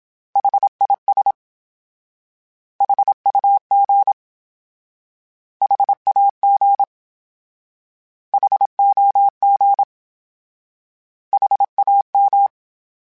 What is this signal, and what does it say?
Morse code, heard as a keyed tone, about 27 wpm, 800 Hz. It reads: HIS HUG HAG HOG HAM
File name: H Prefixes / H mp3s